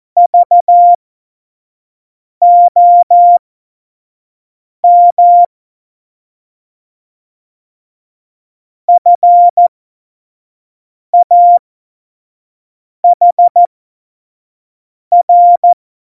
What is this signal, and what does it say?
Morse code, heard as a keyed tone, about 14 wpm, spaced as for 5 wpm, 700 Hz. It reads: VOM FAHR
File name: 25 F70